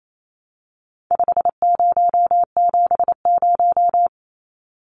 Morse code, heard as a keyed tone, about 28 wpm, 700 Hz. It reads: 5070
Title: Numbers - 4-Digit 28wpm